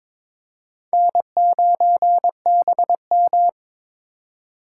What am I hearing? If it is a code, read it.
Morse code, heard as a keyed tone, about 22 wpm, 700 Hz. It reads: N9BM
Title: USA Call Signs - 4 Characters - 8x Character Spacing - 22wpm